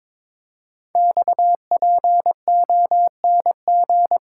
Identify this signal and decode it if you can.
Morse code, heard as a keyed tone, about 23 wpm, 700 Hz. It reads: XPONG
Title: Code Groups - 5 Letters - 22wpm